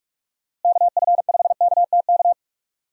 Morse code, heard as a keyed tone, about 42 wpm, 700 Hz. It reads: KF5KTK
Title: USA Call Signs - 6 Characters - Repeated 3x - 45wpm